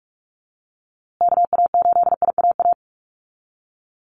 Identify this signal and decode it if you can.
Morse code, heard as a keyed tone, about 44 wpm, 700 Hz. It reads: KA8IAA
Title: USA Call Signs Repeated 3x 45wpm